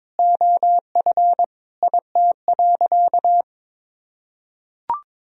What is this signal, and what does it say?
Morse code, heard as a keyed tone, about 22 wpm, 700 Hz. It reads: OF IT.
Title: Sentences from Top 1000 Words - Review Part 2 - 22wpm